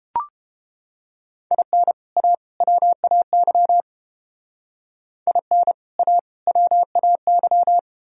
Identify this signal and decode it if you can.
Morse code, heard as a keyed tone, about 30 wpm, 700 Hz. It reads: IN A WAY IN A WAY